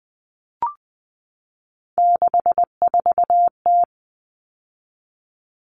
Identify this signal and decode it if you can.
Morse code, heard as a keyed tone, about 20 wpm, 700 Hz. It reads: 64T
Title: Common Cut Numbers - 3-Digit 20wpm